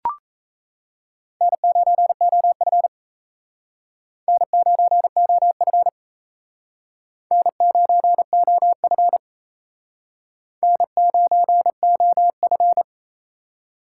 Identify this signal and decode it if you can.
Morse code, heard as a keyed tone, about 35 wpm, 700 Hz. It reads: N9OF N9OF N9OF N9OF
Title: Speed Racing - USA Call Signs 28wpm